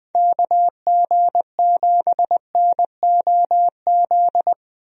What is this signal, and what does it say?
Morse code, heard as a keyed tone, about 21 wpm, 700 Hz. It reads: KG7NOZ
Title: USA Call Signs - 6 Characters - 20wpm